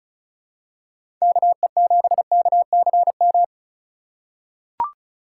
Morse code, heard as a keyed tone, about 36 wpm, 700 Hz. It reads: KE7KCM
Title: USA Call Signs - 3x Character Spacing - 35wpm